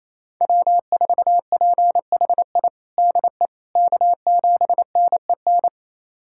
Morse code, heard as a keyed tone, about 28 wpm, 700 Hz. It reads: W4PHI DE K7NEN
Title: USA Callsign DE Callsign - 5 Characters - 28wpm